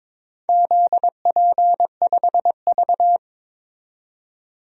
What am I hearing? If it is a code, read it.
Morse code, heard as a keyed tone, about 22 wpm, 700 Hz. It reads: ZP5V
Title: DXCC-100 Call Signs - 4 Characters - 22wpm